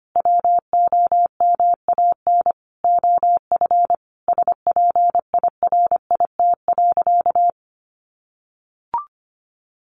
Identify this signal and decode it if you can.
Morse code, heard as a keyed tone, about 25 wpm, 700 Hz. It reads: WOMAN OF SPIRIT.
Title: Sentences from Top 1400 Words 25wpm